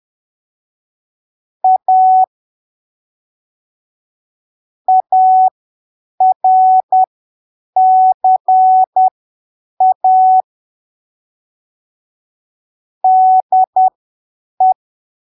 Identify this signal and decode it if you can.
Morse code, heard as a keyed tone, about 10 wpm, spaced as for 7 wpm, 750 Hz. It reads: A ARCA DE